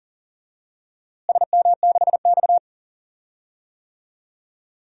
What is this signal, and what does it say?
Morse code, heard as a keyed tone, about 40 wpm, 700 Hz. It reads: SM6X
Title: DXCC-100 Call Signs - 4 Characters - Repeated 2x - 40wpm